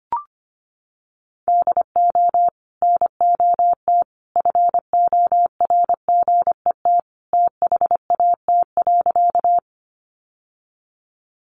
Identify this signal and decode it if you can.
Morse code, heard as a keyed tone, about 25 wpm, 700 Hz. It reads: DO NOT FORGET THAT.
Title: Sentences from Top 1200 Words 25wpm